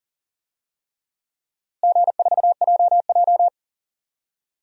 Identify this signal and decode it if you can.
Morse code, heard as a keyed tone, about 40 wpm, 700 Hz. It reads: G4JJ
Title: DXCC-100 Call Signs - 4 Characters - Repeated 3x - 40wpm